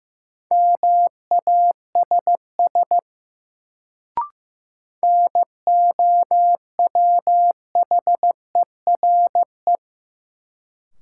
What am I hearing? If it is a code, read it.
Morse code, heard as a keyed tone, about 15 wpm, 700 Hz. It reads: MASS NOWHERE
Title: Sets of 5 Words Encore - New Word Drill - 15wpm